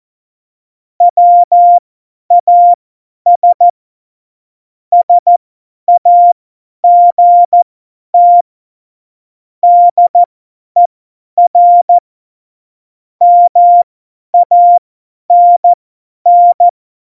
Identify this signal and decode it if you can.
Morse code, heard as a keyed tone, about 14 wpm, spaced as for 10 wpm, 700 Hz. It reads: WAS SAGT DER MANN